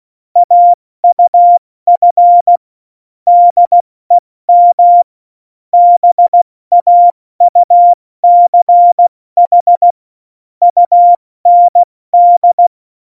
Morse code, heard as a keyed tone, about 16 wpm, 700 Hz. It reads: AUF DEM BAUCH UND
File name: Morsekurs / Witze / Witz 17. Oktober 17, Tempo 70